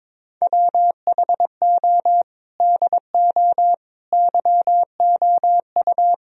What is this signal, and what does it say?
Morse code, heard as a keyed tone, about 22 wpm, 700 Hz. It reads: WHO DO YOU
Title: Sentences from Top 1000 Words - Review Part 4 - 22wpm